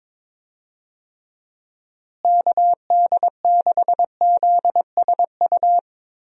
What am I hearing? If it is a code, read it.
Morse code, heard as a keyed tone, about 22 wpm, 700 Hz. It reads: KD6ZSU
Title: USA Call Signs - 6 Characters - Repeated 2x - 22wpm